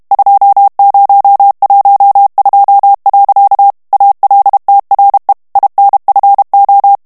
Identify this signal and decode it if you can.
Morse code, heard as a keyed tone, about 32 wpm, 800 Hz. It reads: 2012. ALTRE INFO
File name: morse mp3 / ITALIEN texte clair 1 vitesse 30 mots minute